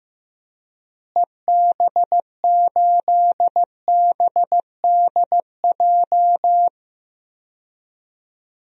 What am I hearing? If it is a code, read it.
Morse code, heard as a keyed tone, about 15 wpm, 700 Hz. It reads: EB8BDJ
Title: DXCC-100 Call Signs - 6 Characters - Repeated 3x - 15wpm